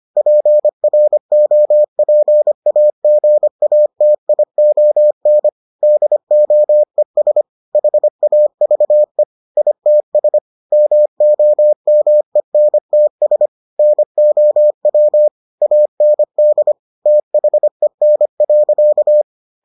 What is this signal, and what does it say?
Morse code, heard as a keyed tone, about 25 wpm, 600 Hz. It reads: PROPAGATION DOES HAVE ITS MOMENTS NOW AND THEN.